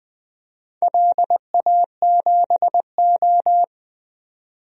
Morse code, heard as a keyed tone, about 20 wpm, 700 Hz. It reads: LA7O
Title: DXCC-100 Call Signs - 4 Characters - 20wpm